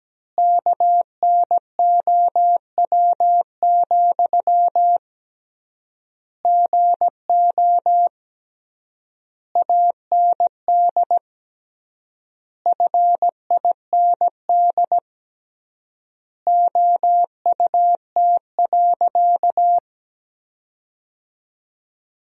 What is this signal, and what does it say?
Morse code, heard as a keyed tone, about 17 wpm, 700 Hz. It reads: KNOW, GO AND FIND OUT.
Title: 3x Word-Spacing - Sentences from Top 200 Words - 17wpm